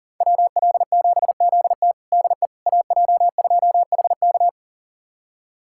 Morse code, heard as a keyed tone, about 40 wpm, 700 Hz. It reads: WL7ZT DE AJ2HK
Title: USA Callsign DE Callsign - 5 Characters - 40wpm